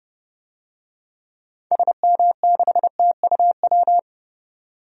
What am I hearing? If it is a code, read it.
Morse code, heard as a keyed tone, about 32 wpm, 700 Hz. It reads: SM6TUW
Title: DXCC-100 Call Signs - 6 Characters - 30wpm